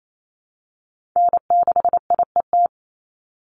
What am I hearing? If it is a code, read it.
Morse code, heard as a keyed tone, about 28 wpm, 700 Hz. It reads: N6IET
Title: Call Signs Repeated 2x - 28wpm